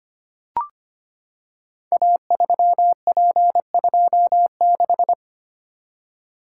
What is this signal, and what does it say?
Morse code, heard as a keyed tone, about 25 wpm, 700 Hz. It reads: A3P26
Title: Code Groups - 5 Letter:Numbers - 25wpm